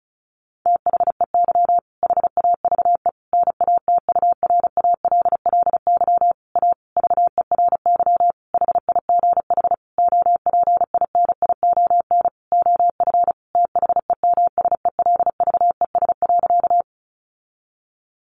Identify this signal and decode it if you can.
Morse code, heard as a keyed tone, about 34 wpm, 700 Hz. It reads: THEY HAVE NATURALLY A VERY HIGH OPINION OF THEMSELVES.
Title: Sentences from Top 1300 Words 35wpm